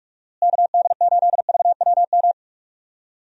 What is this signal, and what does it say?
Morse code, heard as a keyed tone, about 44 wpm, 700 Hz. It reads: KD8VWM
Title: USA Call Signs - 6 Characters - Repeated 2x - 45wpm